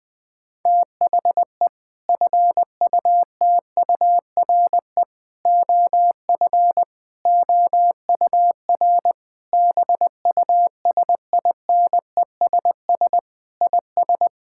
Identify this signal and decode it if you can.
Morse code, heard as a keyed tone, about 20 wpm, 700 Hz. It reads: THE FUTURE OF OUR BUSINESS IS
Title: Sentences from Top 1000 Words - Review Part 6 - 20wpm